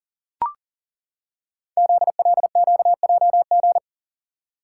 Morse code, heard as a keyed tone, about 40 wpm, 690 Hz. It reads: ZLQJG
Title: Code Groups - 5 Letters - 40wpm